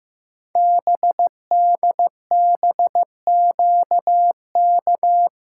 Morse code, heard as a keyed tone, about 15 wpm, 700 Hz. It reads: BDBQK
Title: Code Groups - 5 Letters - 15wpm